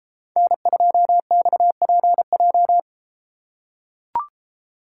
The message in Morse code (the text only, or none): N2XPJ